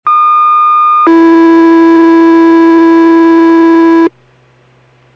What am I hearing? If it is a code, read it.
Two-tone page, heard as a one-second tone, then a three-second tone, about 1220 then 330 Hz, with noise. It is heard from 0.1 to 4.1 s.